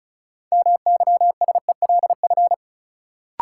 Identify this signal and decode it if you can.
Morse code, heard as a keyed tone, about 36 wpm, 700 Hz. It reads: MYSELF
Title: Sentences from Top 1300 Words - New Word Drill - 35wpm